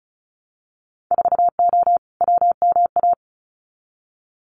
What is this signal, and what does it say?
Morse code, heard as a keyed tone, about 36 wpm, 700 Hz. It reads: 4O WMA
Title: Winter Field Day Exchanges Encore 35wpm